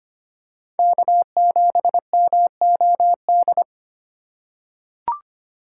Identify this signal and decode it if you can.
Morse code, heard as a keyed tone, about 25 wpm, 700 Hz. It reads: K7MOD
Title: USA Call Signs - 5 Characters - 2x Character Spacing - 25wpm